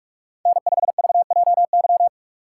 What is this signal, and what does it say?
Morse code, heard as a keyed tone, about 41 wpm, 700 Hz. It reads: N5VJY
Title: USA Call Signs - 5 Characters - 2x Character Spacing - 45wpm